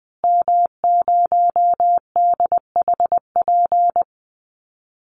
Morse code, heard as a keyed tone, about 20 wpm, 700 Hz. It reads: M0DHP
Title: Call Signs Repeated 2x - 20wpm